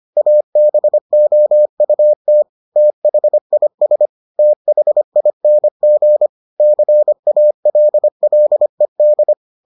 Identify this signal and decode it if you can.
Morse code, heard as a keyed tone, about 25 wpm, 600 Hz. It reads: ABOUT THIS THING CALLED